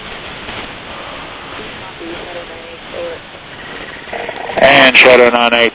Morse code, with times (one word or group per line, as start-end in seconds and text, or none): none